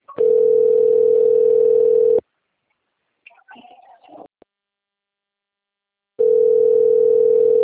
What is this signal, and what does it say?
Ringback tone (North America), 440 + 480 Hz.
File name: rg-1600-unknown- - - .94817